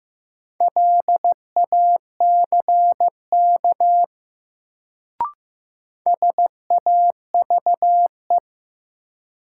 Morse code, heard as a keyed tone, about 15 wpm, 700 Hz. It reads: LACK SAVE
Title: Sentences from Top 1100 Words - New Word Drill - 15wpm